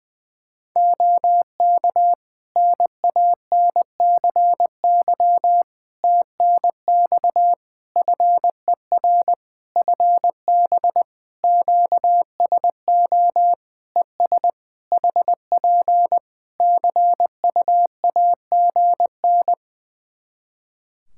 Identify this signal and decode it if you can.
Morse code, heard as a keyed tone, about 20 wpm, 700 Hz. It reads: OK NANCY TNX FER FB QSO ES HP CUAGN
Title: Practice QSO - Exchange 4 - Ending - 20wpm